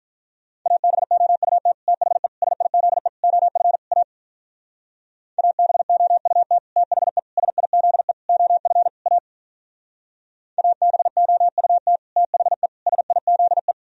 Morse code, heard as a keyed tone, about 42 wpm, 700 Hz. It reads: ABOUT THE SIZE OF A ABOUT THE SIZE OF A ABOUT THE SIZE
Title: Speed-Racing - Sets of 5 Words 35wpm